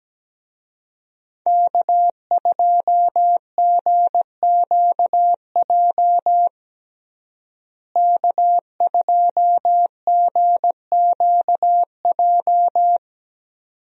Morse code, heard as a keyed tone, about 17 wpm, 700 Hz. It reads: K2GQJ K2GQJ
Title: USA Call Signs - 5 Characters - Repeated 2x - 17wpm